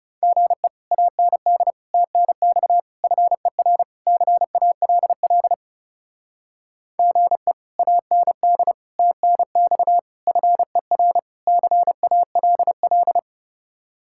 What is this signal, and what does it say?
Morse code, heard as a keyed tone, about 38 wpm, 700 Hz. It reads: GE AND TNX FER CALL GE AND TNX FER CALL